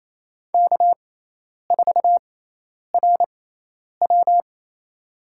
Morse code, heard as a keyed tone, about 28 wpm, spaced as for 10 wpm, 700 Hz. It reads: K4RW
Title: USA Call Signs - 4 Characters - 6x Character Spacing - 28wpm